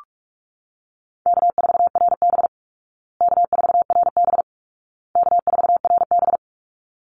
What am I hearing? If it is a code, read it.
Morse code, heard as a keyed tone, about 44 wpm, 700 Hz. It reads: K4RB K4RB K4RB
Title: Easy Call Signs Repeated 3x - 45wpm